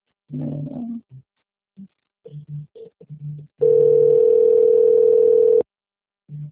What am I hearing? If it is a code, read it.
Ringback tone (North America), 440 + 480 Hz.